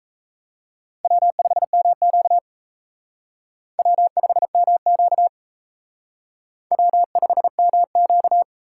Morse code, heard as a keyed tone, about 36 wpm, 700 Hz. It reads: W5MQ W5MQ W5MQ